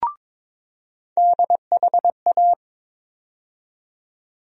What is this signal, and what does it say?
Morse code, heard as a keyed tone, about 21 wpm, 700 Hz. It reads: DHA
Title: USA Call Sign Postfixes - 3 Letters 22wpm